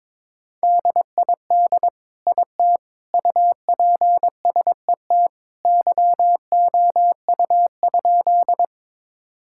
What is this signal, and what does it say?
Morse code, heard as a keyed tone, about 22 wpm, 700 Hz. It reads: DID IT UPSET YOU?
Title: Sentences from Top 1000 Words - Review Part 5 - 22wpm